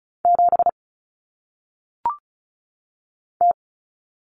Morse code, heard as a keyed tone, about 34 wpm, 700 Hz. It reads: 7 T
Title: Common Cut Numbers - 1-Digit 35wpm